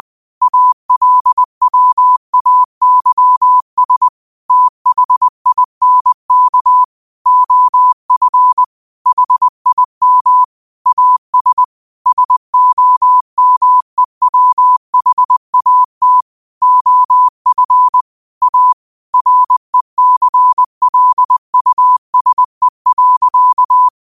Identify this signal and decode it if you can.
Morse code, heard as a keyed tone, about 20 wpm, 1000 Hz. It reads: ALWAYS THINK OF HIM AS SOMEWHAT OF A RECLUSE.
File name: qotdfiles / 20wpm